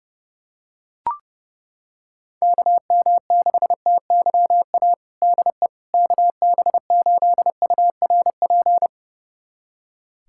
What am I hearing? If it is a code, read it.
Morse code, heard as a keyed tone, about 30 wpm, 700 Hz. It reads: KM6TYA DE KB8URP
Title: USA Callsign DE Callsign 30wpm